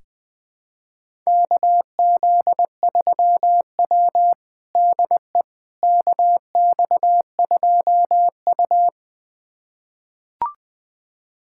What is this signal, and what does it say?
Morse code, heard as a keyed tone, about 20 wpm, 700 Hz. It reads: KZ3W DE KX2U
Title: USA Callsign DE Callsign - 4 Characters - 20wpm